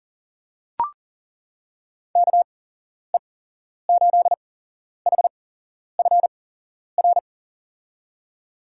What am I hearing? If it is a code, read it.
Morse code, heard as a keyed tone, about 37 wpm, spaced as for 11 wpm, 700 Hz. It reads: KE8HFR
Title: USA Call Signs - 6 Characters - 8x Character Spacing - 40wpm